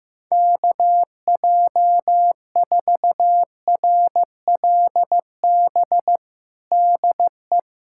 Morse code, heard as a keyed tone, about 15 wpm, 700 Hz. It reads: KJ4RLB DE
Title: USA Callsign DE Callsign - 6 Characters - 15wpm